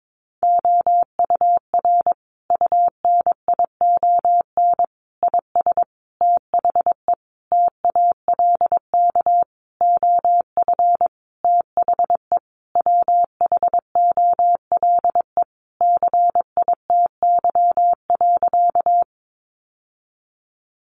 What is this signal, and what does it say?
Morse code, heard as a keyed tone, about 22 wpm, 700 Hz. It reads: OUR UNION IS THE TALK OF THE WHOLE CITY.
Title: Sentences from Top 1400 Words 22wpm